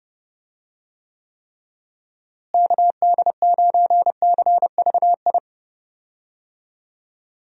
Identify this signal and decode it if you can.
Morse code, heard as a keyed tone, about 31 wpm, 700 Hz. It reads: KD9CVI
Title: USA Call Signs - 6 Characters - Repeated 3x - 30wpm